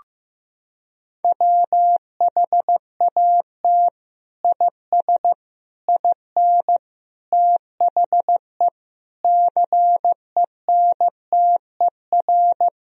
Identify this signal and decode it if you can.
Morse code, heard as a keyed tone, about 16 wpm, 700 Hz. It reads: WHAT IS IN THE CENTER
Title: Sentences from Top 1000 Words - Review Part 2 - 15wpm